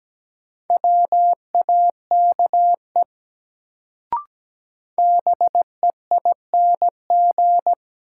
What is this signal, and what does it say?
Morse code, heard as a keyed tone, about 17 wpm, 700 Hz. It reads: WAKE BEING
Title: Sets of 4 Words - New Word Drill - 17wpm